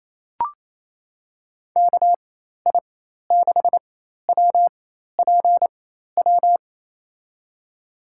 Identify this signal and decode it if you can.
Morse code, heard as a keyed tone, about 28 wpm, spaced as for 13 wpm, 700 Hz. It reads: KI6WPW
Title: USA Call Signs - 6 Characters - 4x Character Spacing - 28wpm